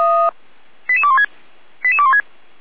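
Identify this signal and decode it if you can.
Five-tone selective call (ZVEI), heard as quick runs of five tones, tones of about 80 ms. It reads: 88216 88216
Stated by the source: ZVEI five-tone code 88399 received by SDR, from the feuerwehr-orlamuende.org (fire brigade) audio server.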